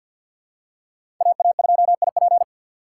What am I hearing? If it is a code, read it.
Morse code, heard as a keyed tone, about 52 wpm, 700 Hz. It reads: AA2IP
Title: USA Call Signs - 5 Characters - 2x Character Spacing - 50wpm